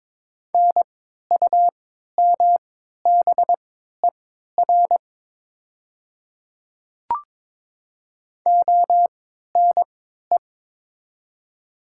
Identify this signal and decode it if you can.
Morse code, heard as a keyed tone, about 22 wpm, spaced as for 12 wpm, 700 Hz. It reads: NUMBER ONE
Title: 3x Character Spacing - Top 100 Words - 22wpm